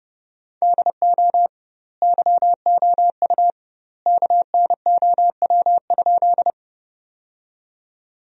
Morse code, heard as a keyed tone, about 31 wpm, 700 Hz. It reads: DO YOU KNOW?
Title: 2x Word-Spacing - Sentences from Top 100 Words - 30wpm